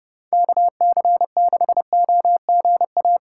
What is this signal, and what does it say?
Morse code, heard as a keyed tone, about 29 wpm, 700 Hz. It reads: KC6OGA